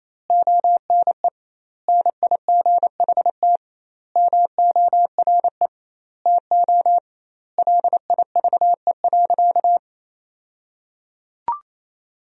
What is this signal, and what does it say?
Morse code, heard as a keyed tone, about 27 wpm, 700 Hz. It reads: ONE NIGHT MORE TO LIVE.
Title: 2x Word-Spacing - Sentences from Top 300 Words - 28wpm